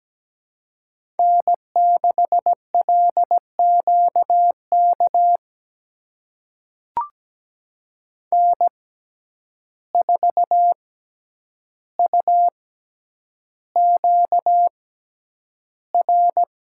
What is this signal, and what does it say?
Morse code, heard as a keyed tone, about 17 wpm, spaced as for 9 wpm, 700 Hz. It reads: N6LQK N4UQR
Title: USA Call Signs - 5 Characters - 6x Character Spacing - 17wpm